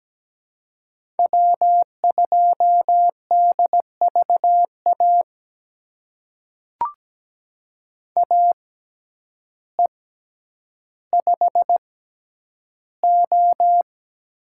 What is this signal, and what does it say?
Morse code, heard as a keyed tone, about 17 wpm, spaced as for 9 wpm, 700 Hz. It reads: W2DVA AE5O
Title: USA Call Signs - 5 Characters - 6x Character Spacing - 17wpm